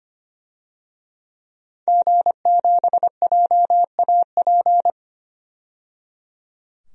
Morse code, heard as a keyed tone, about 25 wpm, 700 Hz. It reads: G7JAP
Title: DXCC-100 Call Signs - 5 Characters - 25wpm